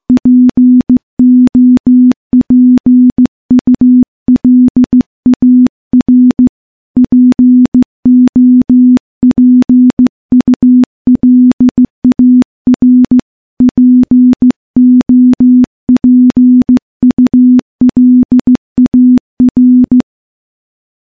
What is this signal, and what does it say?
Morse code, heard as a keyed tone, about 15 wpm, 250 Hz. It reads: POPULAR POPULAR POPULAR
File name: morse / morse-training-15WPM